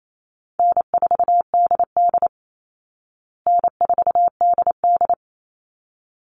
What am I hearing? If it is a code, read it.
Morse code, heard as a keyed tone, about 28 wpm, 700 Hz. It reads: N4DD N4DD
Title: Call Signs Repeated 3x - 28wpm